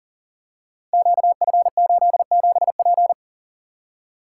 Morse code, heard as a keyed tone, about 42 wpm, 700 Hz. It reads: QF87P
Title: Code Groups - 5 Letter:Numbers - 40wpm